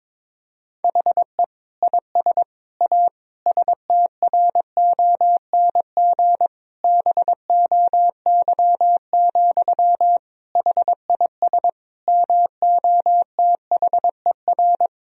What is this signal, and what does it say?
Morse code, heard as a keyed tone, about 21 wpm, 700 Hz. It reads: HE IS A STRONG BOY, HIS MOTHER
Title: Sentences from Top 1000 Words - Review Part 3 - 22wpm